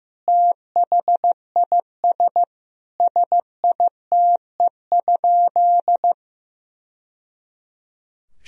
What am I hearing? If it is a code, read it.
Morse code, heard as a keyed tone, about 16 wpm, 700 Hz. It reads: THIS SITE?
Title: Sentences from Top 1000 Words - Review Part 4 - 15wpm